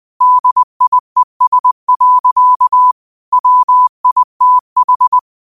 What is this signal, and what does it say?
Morse code, heard as a keyed tone, about 20 wpm, 1000 Hz. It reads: DIES. WITH
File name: qotdfiles / 20wpm